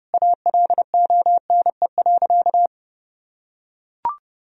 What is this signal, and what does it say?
Morse code, heard as a keyed tone, about 30 wpm, 700 Hz. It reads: ALONE.
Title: Sentences from Top 1000 Words - Review Part 4 - 30wpm